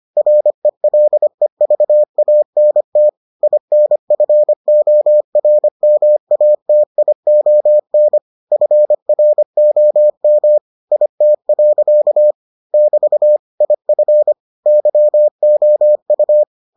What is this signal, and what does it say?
Morse code, heard as a keyed tone, about 25 wpm, 600 Hz. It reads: RELEVANT INFORMATION FROM IT. <BT> IF YOU